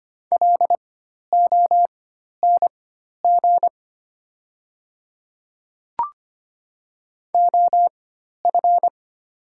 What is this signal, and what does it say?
Morse code, heard as a keyed tone, about 25 wpm, spaced as for 12 wpm, 700 Hz. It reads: LONG OF
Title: 4x Character Spacing - Top 100 Words - 25wpm